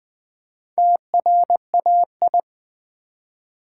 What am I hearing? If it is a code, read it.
Morse code, heard as a keyed tone, about 20 wpm, 700 Hz. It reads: TRAI
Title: Build A Word - 1 Letter At A Time - 3 to 5-Letter Words - 20wpm